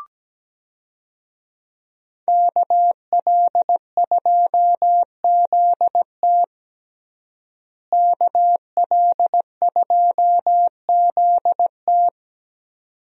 Morse code, heard as a keyed tone, about 17 wpm, 700 Hz. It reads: KL2ZT KL2ZT
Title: USA Call Signs - 5 Characters - Repeated 2x - 17wpm